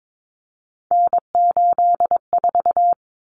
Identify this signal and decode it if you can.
Morse code, heard as a keyed tone, about 22 wpm, 700 Hz. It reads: N84
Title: Common Cut Numbers - 3-Digit 22wpm